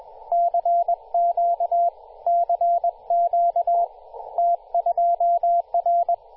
Morse code, heard as a keyed tone, about 21 wpm, 690 Hz, with radio noise. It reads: CQ CQ T2R
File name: t2r-18-071mhz-cw